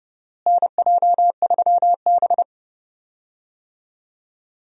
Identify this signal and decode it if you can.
Morse code, heard as a keyed tone, about 30 wpm, 700 Hz. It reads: NJ3B
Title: USA Call Signs - 4 Characters - Repeated 3x - 30wpm